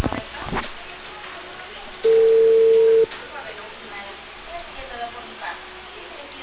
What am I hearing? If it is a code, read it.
Ringback tone (China, 450 Hz), with speech.